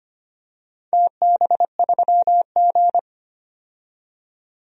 Morse code, heard as a keyed tone, about 25 wpm, 700 Hz. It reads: TB3G
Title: DXCC-100 Call Signs - 4 Characters - 25wpm